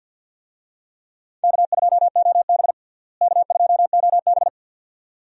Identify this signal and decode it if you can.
Morse code, heard as a keyed tone, about 49 wpm, 700 Hz. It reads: K2OB K2OB
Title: USA Call Signs - 4 Characters - Repeated 2x - 50wpm